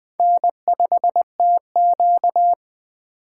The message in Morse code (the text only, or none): N5TQ